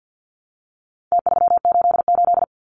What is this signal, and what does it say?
Morse code, heard as a keyed tone, about 50 wpm, 700 Hz. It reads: T388
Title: Numbers - Contest Style - 4-Digit 50wpm